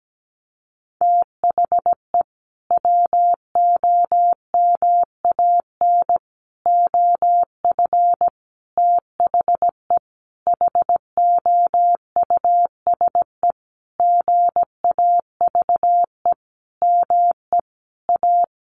Morse code, heard as a keyed tone, about 17 wpm, 700 Hz. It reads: THE WOMAN OF THE HOUSE GAVE ME A
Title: Sentences from Top 1500 Words 17wpm